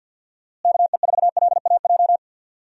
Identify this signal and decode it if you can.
Morse code, heard as a keyed tone, about 52 wpm, 700 Hz. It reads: KE4LAJ
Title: USA Call Signs - 6 Characters - Repeated 3x - 50wpm